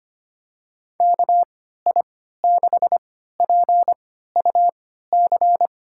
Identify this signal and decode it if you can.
Morse code, heard as a keyed tone, about 25 wpm, spaced as for 14 wpm, 700 Hz. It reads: KI6PUC
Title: USA Call Signs - 3x Character Spacing - 25wpm